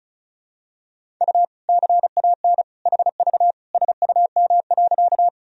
Morse code, heard as a keyed tone, about 34 wpm, 700 Hz. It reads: U CAN HV SUM.
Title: Sentences from Top 100 Words and Abbreviations 35wpm